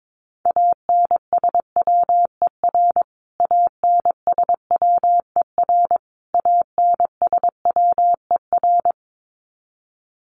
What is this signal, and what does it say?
Morse code, heard as a keyed tone, about 22 wpm, 700 Hz. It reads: ANSWER ANSWER ANSWER
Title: Top 500 Words Repeated 22wpm